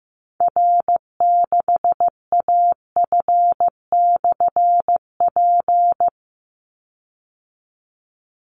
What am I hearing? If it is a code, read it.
Morse code, heard as a keyed tone, about 15 wpm, 700 Hz. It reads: R6AF/P
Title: Call Signs Repeated 3x - 15wpm